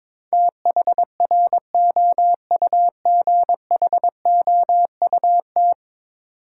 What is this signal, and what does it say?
Morse code, heard as a keyed tone, about 22 wpm, 700 Hz. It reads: THROUGHOUT
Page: morse-code-ninja-practice-22wpm